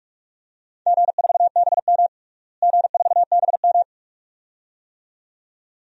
Morse code, heard as a keyed tone, about 42 wpm, 700 Hz. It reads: G4BM G4BM
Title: DXCC-100 Call Signs - 4 Characters - Repeated 2x - 45wpm